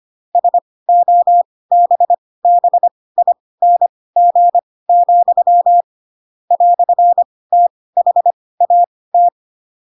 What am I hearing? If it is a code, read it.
Morse code, heard as a keyed tone, about 25 wpm, spaced as for 18 wpm, 700 Hz. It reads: SOBBING, "THAT